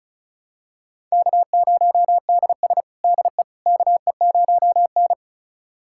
Morse code, heard as a keyed tone, about 36 wpm, 700 Hz. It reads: K0DS DE KE0N
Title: USA Callsign DE Callsign - 4 Characters - 35wpm